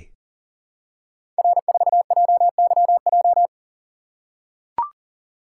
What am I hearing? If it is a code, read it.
Morse code, heard as a keyed tone, about 40 wpm, 700 Hz. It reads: R4JYJ
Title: Code Groups - 5 Letter:Numbers - 40wpm